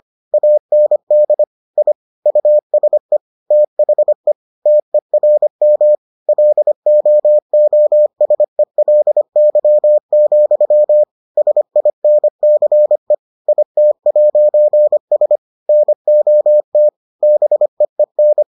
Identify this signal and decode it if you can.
Morse code, heard as a keyed tone, about 25 wpm, 600 Hz. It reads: AND I USE THE TERM LOOSELY, SINCE IT'S NOT BEEN